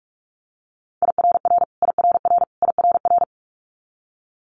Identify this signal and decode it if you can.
Morse code, heard as a keyed tone, about 46 wpm, 700 Hz. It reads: IRR IRR IRR
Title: Multi-letter Phonemes Repeated 3x 45wpm